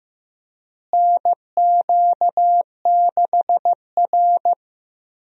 Morse code, heard as a keyed tone, about 16 wpm, 700 Hz. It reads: NQ6R
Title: USA Call Signs - 4 Characters - Repeated 2x - 15wpm